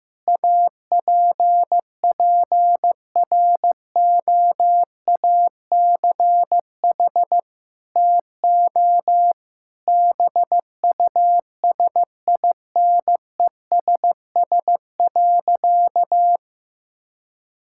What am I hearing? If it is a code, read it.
Morse code, heard as a keyed tone, about 15 wpm, 700 Hz. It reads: APPROACH TO BUSINESS.
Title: Sentences from Top 1000 Words - Review Part 6 - 15wpm